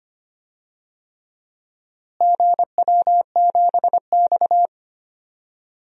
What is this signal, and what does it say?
Morse code, heard as a keyed tone, about 25 wpm, 700 Hz. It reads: GW7X
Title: DXCC-100 Call Signs - 4 Characters - Repeated 3x - 25wpm